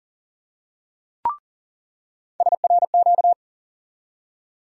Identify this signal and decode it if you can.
Morse code, heard as a keyed tone, about 40 wpm, 700 Hz. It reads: SRQ
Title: USA Call Sign Postfixes - 3 Letters 40wpm